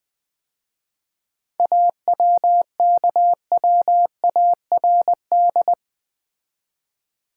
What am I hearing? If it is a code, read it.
Morse code, heard as a keyed tone, about 20 wpm, 700 Hz. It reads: AWKWARD
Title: 7 Letter Words - First 3 Letters Given - 20wpm